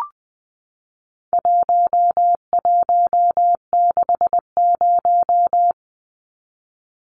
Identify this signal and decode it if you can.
Morse code, heard as a keyed tone, about 20 wpm, 700 Hz. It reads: 1160
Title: Numbers - 4-Digit 20wpm